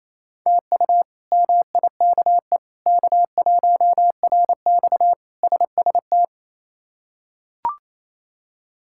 Morse code, heard as a keyed tone, about 27 wpm, 700 Hz. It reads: TU MIKE K1RX SST
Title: K1USN SST QSO Practice - 28wpm